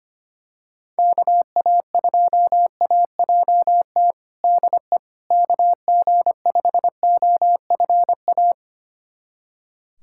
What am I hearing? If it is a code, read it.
Morse code, heard as a keyed tone, about 25 wpm, 700 Hz. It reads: KA2AJT DE KG5OFA
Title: USA Callsign DE Callsign - 6 Characters - 25wpm